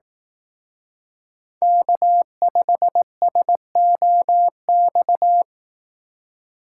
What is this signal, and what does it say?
Morse code, heard as a keyed tone, about 18 wpm, 700 Hz. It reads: K5SOX
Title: Speed Racing - USA 1x3 Call Signs 15wpm